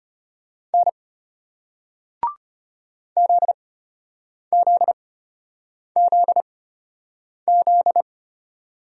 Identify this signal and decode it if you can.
Morse code, heard as a keyed tone, about 36 wpm, spaced as for 8 wpm, 700 Hz. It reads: N ZZZZ